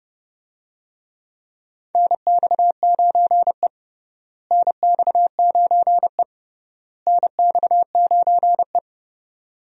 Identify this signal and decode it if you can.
Morse code, heard as a keyed tone, about 30 wpm, 700 Hz. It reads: NX9E NX9E NX9E
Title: USA Call Signs - 4 Characters - Repeated 3x - 30wpm